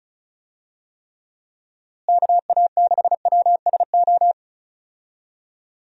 Morse code, heard as a keyed tone, about 34 wpm, 700 Hz. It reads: KA6WSO
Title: USA Call Signs - 6 Characters - Repeated 2x - 35wpm